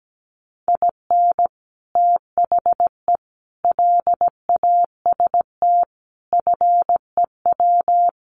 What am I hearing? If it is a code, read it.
Morse code, heard as a keyed tone, about 17 wpm, 700 Hz. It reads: IN THE LAST FEW
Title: Sets of 5 Words Encore2 17wpm